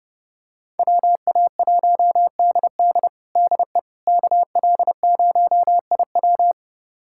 Morse code, heard as a keyed tone, about 32 wpm, 700 Hz. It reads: WA1DD DE KL0IW
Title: USA Callsign DE Callsign - 5 Characters - 30wpm